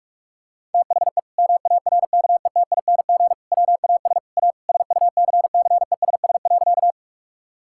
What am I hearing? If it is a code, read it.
Morse code, heard as a keyed tone, about 46 wpm, 700 Hz. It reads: THE MARKETING WAS A SUCCESS.
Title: Sentences from Top 1000 Words - Review Part 5 - 45wpm